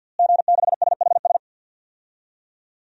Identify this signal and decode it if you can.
Morse code, heard as a keyed tone, about 52 wpm, 700 Hz. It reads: G6SHS
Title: DXCC-100 Call Signs - 5 Characters - Repeated 3x - 50wpm